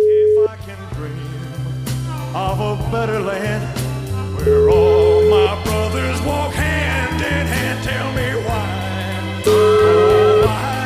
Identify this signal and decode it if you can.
Ringback tone (Europe, 425 Hz), with speech.